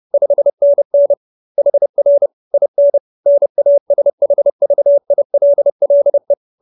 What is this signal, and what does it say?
Morse code, heard as a keyed tone, about 30 wpm, 570 Hz. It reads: 5NN HR IN NASHVILLE